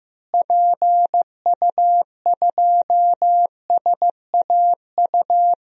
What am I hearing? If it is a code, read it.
Morse code, heard as a keyed tone, about 15 wpm, 700 Hz. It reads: PU2SAU
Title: DXCC-100 Call Signs - 6 Characters - Repeated 2x - 15wpm